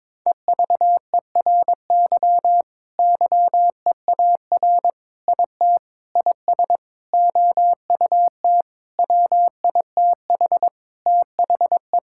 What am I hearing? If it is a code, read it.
Morse code, heard as a keyed tone, about 21 wpm, 700 Hz. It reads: EVERY YEAR IT IS OUT WITH THE
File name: Sentences with 9-Word Phrases - 22wpm